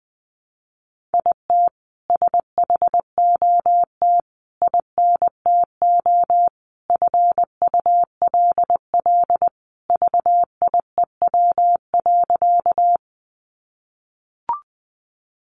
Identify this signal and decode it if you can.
Morse code, heard as a keyed tone, about 20 wpm, 700 Hz. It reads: IT SHOT INTO FULL VIEW.
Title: Sentences from Top 1100 Words 20wpm